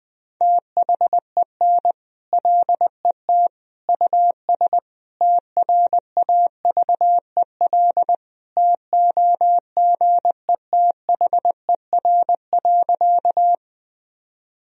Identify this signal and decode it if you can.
Morse code, heard as a keyed tone, about 20 wpm, 700 Hz. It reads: THEN LET US TRAVEL TOGETHER.
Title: Sentences from Top 1000 Words - Review Part 2 - 20wpm